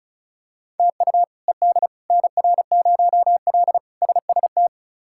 Morse code, heard as a keyed tone, about 34 wpm, 700 Hz. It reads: TU ED NR0L SST
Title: K1USN SST QSO Practice - 35wpm